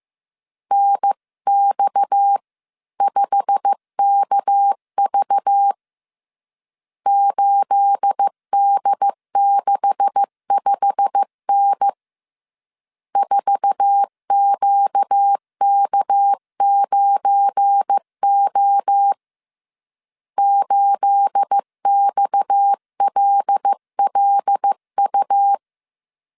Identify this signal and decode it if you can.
Morse code, heard as a keyed tone, about 15 wpm, 800 Hz. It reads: NX5KV 8D65N 4QK9O 8XLLU
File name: CW / DET / 1 9 1500C S A 14wpm 3222